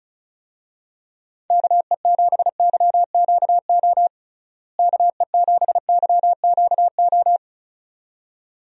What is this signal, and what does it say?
Morse code, heard as a keyed tone, about 36 wpm, 700 Hz. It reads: KE7YQO KE7YQO
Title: USA Call Signs - 6 Characters - Repeated 2x - 35wpm